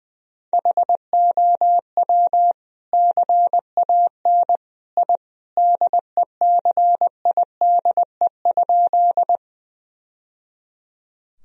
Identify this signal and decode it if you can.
Morse code, heard as a keyed tone, about 20 wpm, 700 Hz. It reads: HOW CAN I DECIDE?
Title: Sentences from Top 1000 Words - Review Part 3 - 20wpm